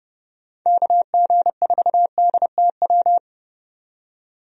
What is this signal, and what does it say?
Morse code, heard as a keyed tone, about 30 wpm, 700 Hz. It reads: KG4DTW
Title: USA Call Signs - 6 Characters - 30wpm